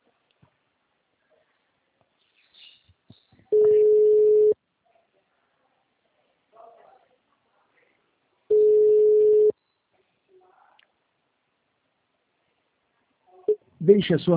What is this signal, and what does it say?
Ringback tone (Europe, 425 Hz), with speech.
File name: out- -2019- - - .11990